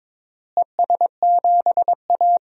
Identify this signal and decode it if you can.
Morse code, heard as a keyed tone, about 22 wpm, 700 Hz. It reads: ES7A
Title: DXCC-100 Call Signs - 4 Characters - Repeated 3x - 22wpm